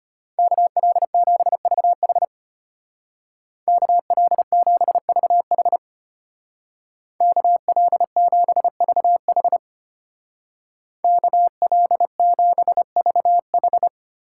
Morse code, heard as a keyed tone, about 30 wpm, 700 Hz. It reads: KL7VH KL7VH KL7VH KL7VH
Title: Speed Racing - USA 2x2 Call Signs 25wpm